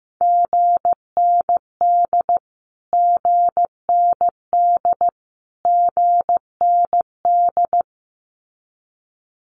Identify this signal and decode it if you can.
Morse code, heard as a keyed tone, about 15 wpm, 700 Hz. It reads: GND GND GND
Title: Abbreviations Repeated 3x - 15wpm